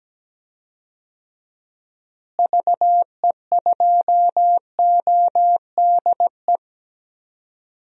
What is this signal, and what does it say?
Morse code, heard as a keyed tone, about 17 wpm, 700 Hz. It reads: VE2ODE
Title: DXCC-100 Call Signs - 6 Characters - Repeated 2x - 17wpm